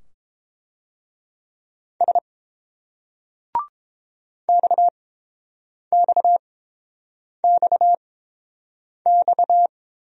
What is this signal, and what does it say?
Morse code, heard as a keyed tone, about 30 wpm, spaced as for 7 wpm, 700 Hz. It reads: S XXXX